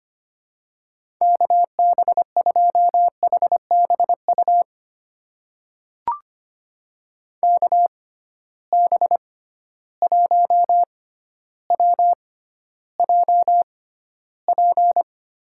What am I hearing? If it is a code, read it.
Morse code, heard as a keyed tone, about 25 wpm, spaced as for 13 wpm, 700 Hz. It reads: KB2HBU KB1WJP
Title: USA Call Signs - 6 Characters - 6x Character Spacing - 25wpm